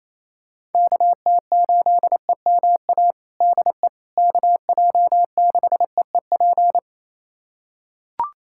Morse code, heard as a keyed tone, about 28 wpm, 700 Hz. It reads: KT8EMA DE KJ6EEP
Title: USA Callsign DE Callsign 28wpm